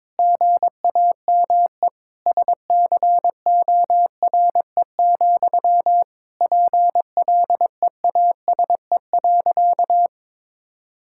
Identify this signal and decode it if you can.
Morse code, heard as a keyed tone, about 21 wpm, 700 Hz. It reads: GAME SCORE, PLEASE.
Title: Sentences from Top 1600 Words 22wpm